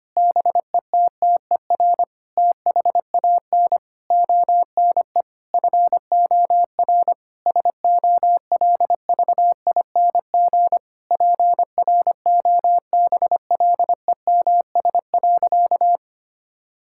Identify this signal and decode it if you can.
Morse code, heard as a keyed tone, about 25 wpm, 700 Hz. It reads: BETTER THAN ONE FOR SOLVING PROBLEMS.
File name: practice / 25wpm / Difficult Sentences with Hints - 25wpm